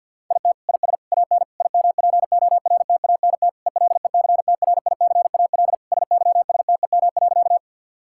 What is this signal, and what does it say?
Morse code, heard as a keyed tone, about 50 wpm, 700 Hz. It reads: IT IS AN IMPORTANT ELECTRICAL SYSTEM.
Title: Sentences from Top 1000 Words - Review Part 6 - 50wpm